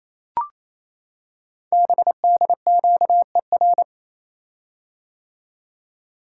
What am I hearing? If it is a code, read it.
Morse code, heard as a keyed tone, about 27 wpm, 700 Hz. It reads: BDQER
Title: Code Groups - 5 Letters - 28wpm